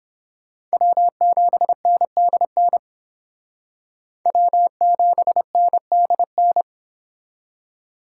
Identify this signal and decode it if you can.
Morse code, heard as a keyed tone, about 30 wpm, 700 Hz. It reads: W7NDN W7NDN